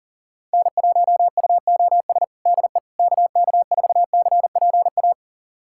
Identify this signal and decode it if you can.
Morse code, heard as a keyed tone, about 37 wpm, 700 Hz. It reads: N1UOS DE KK4CPA
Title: USA Callsign DE Callsign 40wpm